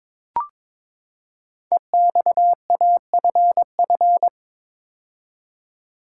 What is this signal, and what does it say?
Morse code, heard as a keyed tone, about 22 wpm, 700 Hz. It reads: EXAFF
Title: Code Groups - 5 Letters - 22wpm